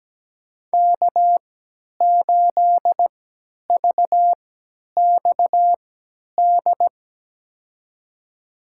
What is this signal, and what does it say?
Morse code, heard as a keyed tone, about 17 wpm, spaced as for 10 wpm, 700 Hz. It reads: K8VXD
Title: USA Call Signs - 5 Characters - 3x Character Spacing - 17wpm